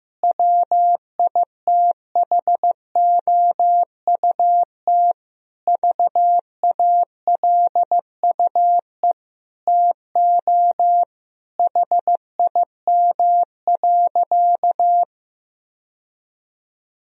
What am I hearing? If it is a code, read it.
Morse code, heard as a keyed tone, about 15 wpm, 700 Hz. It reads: WITHOUT VALUE TO HIM.
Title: Sentences from Top 1000 Words - Review Part 3 - 15wpm